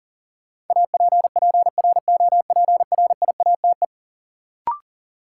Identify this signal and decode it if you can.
Morse code, heard as a keyed tone, about 37 wpm, 700 Hz. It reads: APPROPRIATE
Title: Sentences from Top 900 Words - New Word Drill - 40wpm